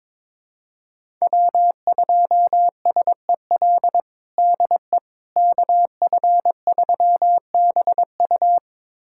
Morse code, heard as a keyed tone, about 21 wpm, 700 Hz. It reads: W2SEL DE KF3BU
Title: USA Callsign DE Callsign - 5 Characters - 22wpm